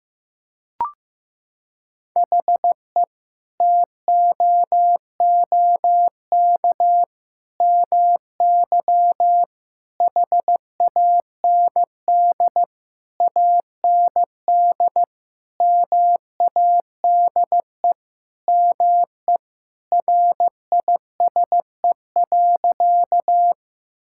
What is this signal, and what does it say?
Morse code, heard as a keyed tone, about 16 wpm, 700 Hz. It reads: HE TOOK MY HAND AND MADE ME RISE.
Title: Sentences from Top 1000 Words - Review Part 6 - 15wpm